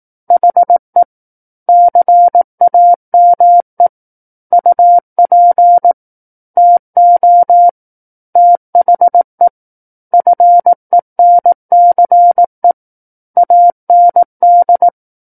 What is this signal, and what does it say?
Morse code, heard as a keyed tone, about 18 wpm, 700 Hz. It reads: HE CAME UP TO THE FENCE AND
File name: 18 WPM